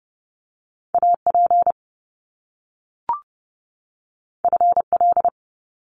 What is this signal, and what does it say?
Morse code, heard as a keyed tone, about 30 wpm, 700 Hz. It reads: A P F L
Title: Top Two Letter Combinations in Words 30wpm